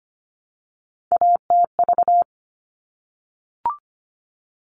Morse code, heard as a keyed tone, about 25 wpm, 700 Hz. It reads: ATV
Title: All Cut Numbers - 3-Digit 25wpm